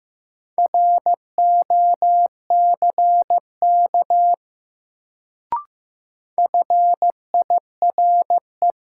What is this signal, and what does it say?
Morse code, heard as a keyed tone, about 15 wpm, 700 Hz. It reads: ROCK FIRE
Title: Sentences from Top 400 Words - New Word Drill - 15wpm